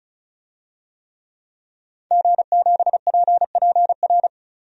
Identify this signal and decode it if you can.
Morse code, heard as a keyed tone, about 36 wpm, 700 Hz. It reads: G7PPR
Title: DXCC-100 Call Signs - 5 Characters - 35wpm